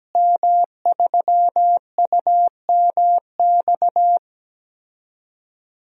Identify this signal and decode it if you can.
Morse code, heard as a keyed tone, about 17 wpm, 700 Hz. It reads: M3UMX
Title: DXCC-100 Call Signs - 5 Characters - Repeated 3x - 17wpm